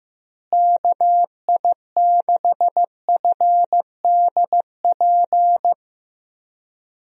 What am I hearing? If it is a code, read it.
Morse code, heard as a keyed tone, about 15 wpm, 700 Hz. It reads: KI6FDP
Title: USA Call Signs - 6 Characters - Repeated 3x - 15wpm